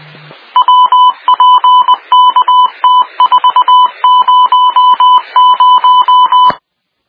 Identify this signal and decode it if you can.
Morse code, heard as a keyed tone, about 20 wpm, 1000 Hz. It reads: WPKT400